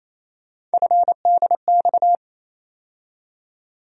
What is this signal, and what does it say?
Morse code, heard as a keyed tone, about 28 wpm, 700 Hz. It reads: FDX
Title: USA Call Sign Postfixes - 3 Letters 28wpm